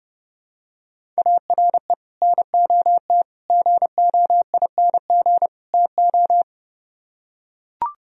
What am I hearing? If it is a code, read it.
Morse code, heard as a keyed tone, about 30 wpm, 700 Hz. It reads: ARE NOT GOING TO